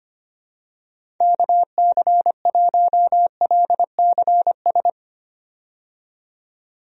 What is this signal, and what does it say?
Morse code, heard as a keyed tone, about 25 wpm, 700 Hz. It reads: KC1LCS
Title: USA Call Signs - 6 Characters - 25wpm